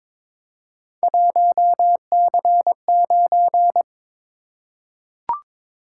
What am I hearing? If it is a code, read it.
Morse code, heard as a keyed tone, about 22 wpm, 700 Hz. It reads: 1C9
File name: practice / 22wpm / Three Letter-Number Combinations - 22wpm